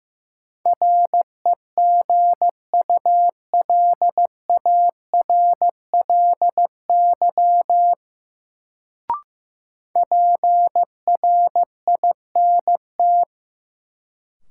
Morse code, heard as a keyed tone, about 15 wpm, 700 Hz. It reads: REGULARLY PRINT
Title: Sentences from Top 1400 Words - New Word Drill - 15wpm